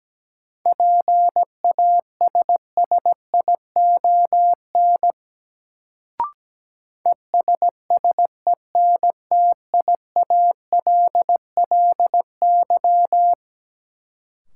Morse code, heard as a keyed tone, about 17 wpm, 700 Hz. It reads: PASSION ESSENTIALLY
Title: Sentences from Top 1300 Words - New Word Drill - 17wpm